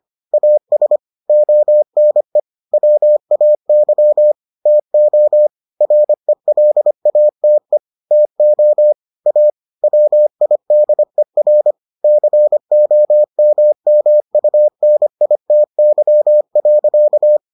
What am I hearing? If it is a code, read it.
Morse code, heard as a keyed tone, about 25 wpm, 600 Hz. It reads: AS ONE WAY TO RELATE TO A WIDER COMMUNITY.